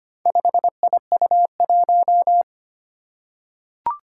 Morse code, heard as a keyed tone, about 25 wpm, 700 Hz. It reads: 5IU1
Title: Four Challenging Letter-Number Combinations 25wpm